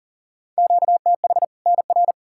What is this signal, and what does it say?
Morse code, heard as a keyed tone, about 37 wpm, 700 Hz. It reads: QTH NR
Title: Practice QSO - Exchange 1 - Beginning - 40wpm